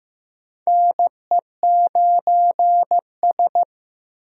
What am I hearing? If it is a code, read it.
Morse code, heard as a keyed tone, about 15 wpm, 700 Hz. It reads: NE9S
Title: USA Call Signs - 4 Characters - Repeated 2x - 15wpm